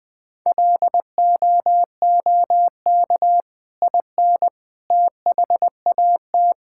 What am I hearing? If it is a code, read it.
Morse code, heard as a keyed tone, about 19 wpm, 700 Hz. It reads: LOOK IN THAT
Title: Sentences from Top 1000 Words - Review Part 5 - 20wpm